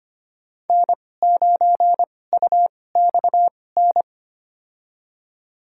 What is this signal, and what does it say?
Morse code, heard as a keyed tone, about 25 wpm, spaced as for 18 wpm, 700 Hz. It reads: N9UXN
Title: USA Call Signs - 5 Characters - 2x Character Spacing - 25wpm